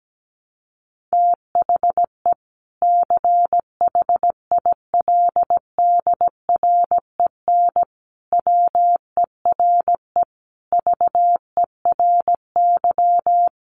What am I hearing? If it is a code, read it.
Morse code, heard as a keyed tone, about 17 wpm, 700 Hz. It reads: THE CHILDREN WERE VERY
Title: Sentences from Top 1300 Words 17wpm